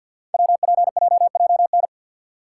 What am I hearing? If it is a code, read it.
Morse code, heard as a keyed tone, about 51 wpm, 690 Hz. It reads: WPJJN
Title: Code Groups - 5 Letters - 50wpm